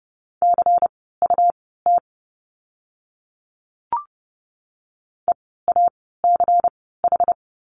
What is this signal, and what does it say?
Morse code, heard as a keyed tone, about 30 wpm, spaced as for 17 wpm, 700 Hz. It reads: CUT EACH
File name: Lesson 17f - Letter u - Words 30wpm